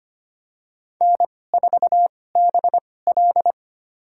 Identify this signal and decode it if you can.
Morse code, heard as a keyed tone, about 25 wpm, spaced as for 18 wpm, 700 Hz. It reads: N4BL
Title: USA Call Signs - 4 Characters - 2x Character Spacing - 25wpm